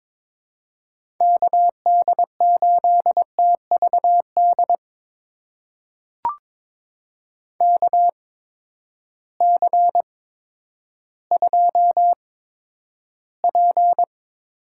Morse code, heard as a keyed tone, about 22 wpm, spaced as for 11 wpm, 700 Hz. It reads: KD8TVD KC2P
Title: USA Call Signs - 6 Characters - 8x Character Spacing - 22wpm